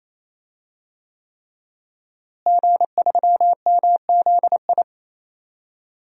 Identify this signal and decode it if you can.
Morse code, heard as a keyed tone, about 28 wpm, 700 Hz. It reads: G3MZI
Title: DXCC-100 Call Signs - 5 Characters - Repeated 3x - 28wpm